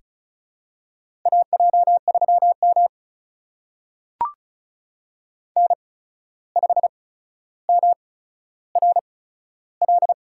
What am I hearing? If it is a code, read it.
Morse code, heard as a keyed tone, about 36 wpm, spaced as for 14 wpm, 700 Hz. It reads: AJ3M N5MRL
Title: USA Call Signs - 8x Character Spacing - 35wpm